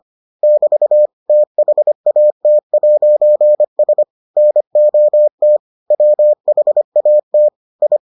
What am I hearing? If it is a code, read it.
Morse code, heard as a keyed tone, about 25 wpm, 600 Hz. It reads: <BT> THAT'S NOT WHAT I